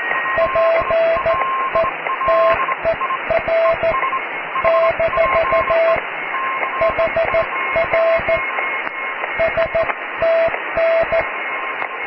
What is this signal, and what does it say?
Morse code, heard as a keyed tone, about 14 wpm, spaced as for 12 wpm, 660 Hz, with radio noise. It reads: PETER - HR STN
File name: cw / QSO+QRM